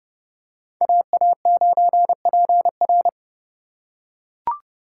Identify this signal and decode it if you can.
Morse code, heard as a keyed tone, about 29 wpm, 700 Hz. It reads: AA9PR